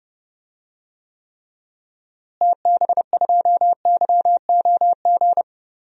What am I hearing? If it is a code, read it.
Morse code, heard as a keyed tone, about 30 wpm, 700 Hz. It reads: TB2YOG
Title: DXCC-100 Call Signs - 6 Characters - Repeated 2x - 30wpm